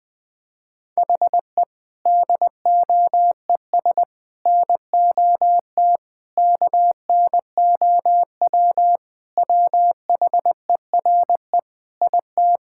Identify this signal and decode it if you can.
Morse code, heard as a keyed tone, about 20 wpm, 700 Hz. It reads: HE DOES NOT KNOW WHERE IT
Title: Sentences with 4-Word Phrases - 20wpm